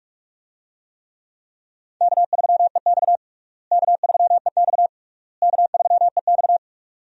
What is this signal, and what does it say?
Morse code, heard as a keyed tone, about 46 wpm, 700 Hz. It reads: K3EX K3EX K3EX
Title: USA Call Signs - 4 Characters - Repeated 3x - 45wpm